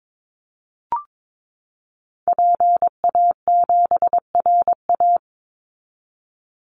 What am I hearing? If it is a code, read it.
Morse code, heard as a keyed tone, about 22 wpm, 690 Hz. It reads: PA7RA
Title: Call Signs Repeated 3x - 22wpm